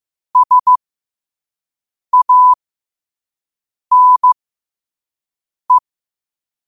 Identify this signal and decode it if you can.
Morse code, heard as a keyed tone, about 15 wpm, spaced as for 5 wpm, 1000 Hz. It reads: SANE
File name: qotdfiles / 5wpm